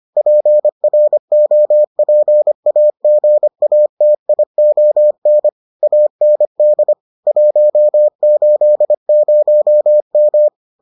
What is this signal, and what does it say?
Morse code, heard as a keyed tone, about 25 wpm, 600 Hz. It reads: PROPAGATION AND 180M